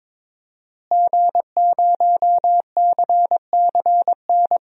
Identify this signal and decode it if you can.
Morse code, heard as a keyed tone, about 22 wpm, 700 Hz. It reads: G0CCN
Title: DXCC-100 Call Signs - 5 Characters - Repeated 3x - 22wpm